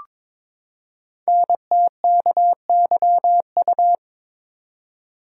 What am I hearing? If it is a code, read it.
Morse code, heard as a keyed tone, about 22 wpm, 700 Hz. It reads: NTKYU
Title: Code Groups - 5 Letters - 22wpm